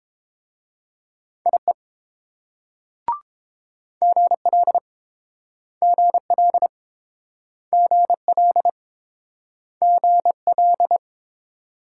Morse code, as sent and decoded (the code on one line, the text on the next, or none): .. / --. .-.. / --. .-.. / --. .-.. / --. .-..
IE GL GL GL GL